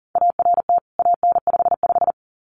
Morse code, heard as a keyed tone, about 40 wpm, 700 Hz. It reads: ART AN55
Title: CWT Exchange 40wpm